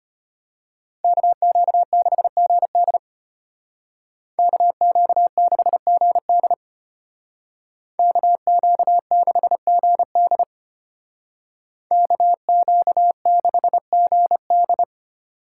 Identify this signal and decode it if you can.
Morse code, heard as a keyed tone, about 30 wpm, 700 Hz. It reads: KQ6GD KQ6GD KQ6GD KQ6GD
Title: Speed Racing - USA Call Signs 25wpm